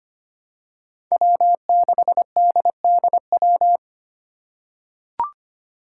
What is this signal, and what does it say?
Morse code, heard as a keyed tone, about 25 wpm, 700 Hz. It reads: W6DDW
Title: Five Challenging Letter-Number Combinations 25wpm